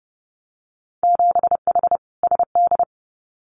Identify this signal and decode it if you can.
Morse code, heard as a keyed tone, about 30 wpm, 700 Hz. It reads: 7H SD
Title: Winter Field Day Exchanges Encore 30wpm